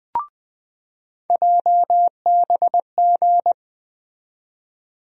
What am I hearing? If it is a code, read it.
Morse code, heard as a keyed tone, about 20 wpm, 700 Hz. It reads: JBG
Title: USA Call Sign Postfixes - 3 Letters 20wpm